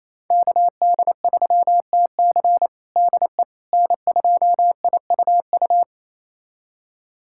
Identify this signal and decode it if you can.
Morse code, heard as a keyed tone, about 28 wpm, 700 Hz. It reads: KD3TC DE N2IUU
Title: USA Callsign DE Callsign - 5 Characters - 28wpm